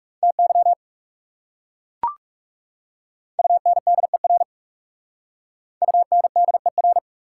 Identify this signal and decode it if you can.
Morse code, heard as a keyed tone, about 39 wpm, 700 Hz. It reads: TY UNDER UNDER